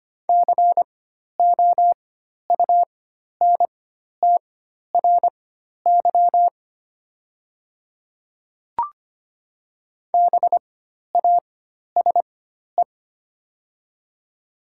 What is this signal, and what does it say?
Morse code, heard as a keyed tone, about 25 wpm, spaced as for 12 wpm, 700 Hz. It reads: COUNTRY BASE
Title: 4x Character Spacing - Top 300 Words - 25wpm